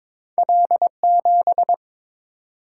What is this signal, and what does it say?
Morse code, heard as a keyed tone, about 23 wpm, 700 Hz. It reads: L7
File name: practice / 22wpm / Two Letter-Number Combinations - 22wpm